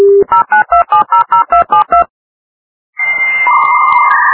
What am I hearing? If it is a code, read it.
DTMF, heard as telephone touch tones, tones of about 120 ms. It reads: 092700272